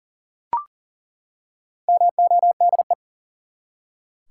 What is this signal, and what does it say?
Morse code, heard as a keyed tone, about 40 wpm, 700 Hz. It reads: MODE
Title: Four Letter Words - 40wpm